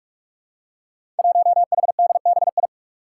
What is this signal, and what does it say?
Morse code, heard as a keyed tone, about 44 wpm, 700 Hz. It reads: 1HDBI
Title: Five Challenging Letter-Number Combinations 45wpm